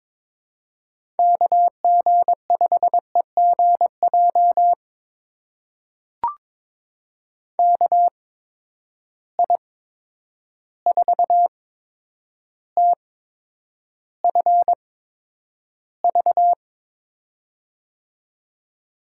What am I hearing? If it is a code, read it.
Morse code, heard as a keyed tone, about 22 wpm, spaced as for 9 wpm, 700 Hz. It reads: KG5EGJ KI4TFV
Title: USA Call Signs - 8x Character Spacing - 22wpm